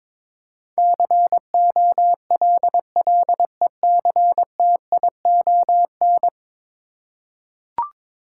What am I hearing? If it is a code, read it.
Morse code, heard as a keyed tone, about 22 wpm, 700 Hz. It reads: COLLECTION
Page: morse-code-ninja-practice-22wpm